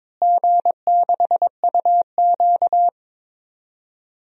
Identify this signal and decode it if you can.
Morse code, heard as a keyed tone, about 22 wpm, 700 Hz. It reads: G6UQ
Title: DXCC-100 Call Signs - 4 Characters - Repeated 2x - 22wpm